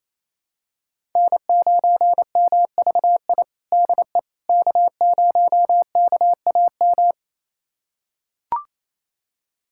Morse code, heard as a keyed tone, about 27 wpm, 700 Hz. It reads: N9MVI DE K0KAM
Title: USA Callsign DE Callsign - 5 Characters - 28wpm